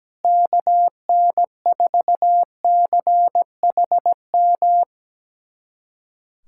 Morse code, heard as a keyed tone, about 17 wpm, 700 Hz. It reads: KN4CHM
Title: USA Callsign DE Callsign - 6 Characters - 17wpm